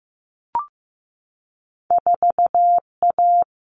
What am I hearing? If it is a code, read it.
Morse code, heard as a keyed tone, about 15 wpm, 700 Hz. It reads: 4A
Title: Common Cut Numbers - 2-Digit 15wpm